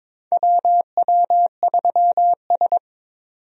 Morse code, heard as a keyed tone, about 22 wpm, 700 Hz. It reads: WW3S
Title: USA Call Signs - 4 Characters - 22wpm